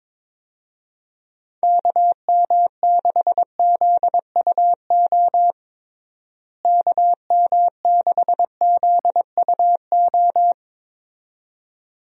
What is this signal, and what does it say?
Morse code, heard as a keyed tone, about 22 wpm, 700 Hz. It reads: KM6ZUO KM6ZUO
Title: USA Call Signs - 6 Characters - Repeated 2x - 22wpm